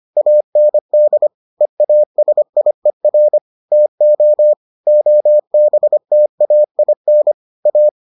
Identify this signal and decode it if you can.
Morse code, heard as a keyed tone, about 25 wpm, 600 Hz. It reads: AND EASIER TO OBTAIN A